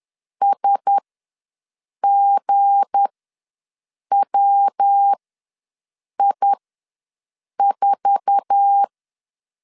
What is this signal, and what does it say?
Morse code, heard as a keyed tone, about 10 wpm, spaced as for 6 wpm, 800 Hz. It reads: SGWI4
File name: LEZIONI / 9 286 ws 10wpm 4032